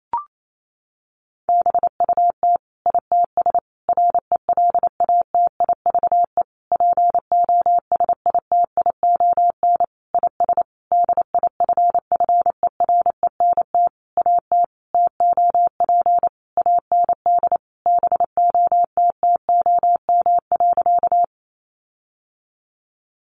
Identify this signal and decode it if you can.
Morse code, heard as a keyed tone, about 28 wpm, 700 Hz. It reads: BUT ITS RELATIVE POSITION IS DIFFERENT AT TOP AND BOTTOM.
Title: Sentences from Top 1400 Words 28wpm